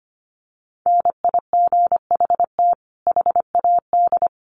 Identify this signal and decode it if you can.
Morse code, heard as a keyed tone, about 25 wpm, 700 Hz. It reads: NIGHT HAD
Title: Sentences from Top 1300 Words 25wpm